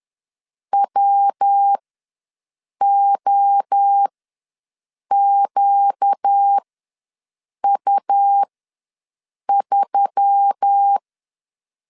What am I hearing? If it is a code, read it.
Morse code, heard as a keyed tone, about 10 wpm, spaced as for 6 wpm, 800 Hz. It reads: WOQU3
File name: LEZIONI / 5 LBVC / 5 L ws 10wpm 3636